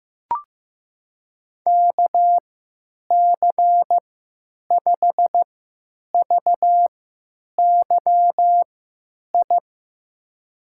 Morse code, heard as a keyed tone, about 15 wpm, spaced as for 9 wpm, 700 Hz. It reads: KC5VYI
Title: USA Call Signs - 6 Characters - 3x Character Spacing - 15wpm